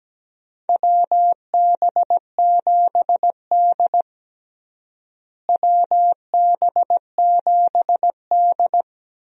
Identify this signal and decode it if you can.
Morse code, heard as a keyed tone, about 17 wpm, 700 Hz. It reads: WB7D WB7D
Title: USA Call Signs - 4 Characters - Repeated 3x - 17wpm